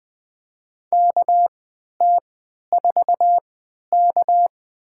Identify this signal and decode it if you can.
Morse code, heard as a keyed tone, about 20 wpm, spaced as for 11 wpm, 700 Hz. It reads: KT4K
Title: USA Call Signs - 4 Characters - 3x Character Spacing - 20wpm